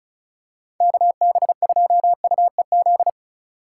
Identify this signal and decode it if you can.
Morse code, heard as a keyed tone, about 36 wpm, 700 Hz. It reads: KB2UEZ
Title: USA Call Signs - 6 Characters - 8x Character Spacing - 35wpm